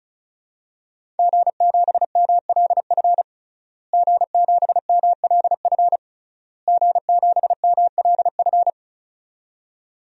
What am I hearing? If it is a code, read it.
Morse code, heard as a keyed tone, about 34 wpm, 700 Hz. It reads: G7MLF G7MLF G7MLF
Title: DXCC-100 Call Signs - 5 Characters - Repeated 3x - 35wpm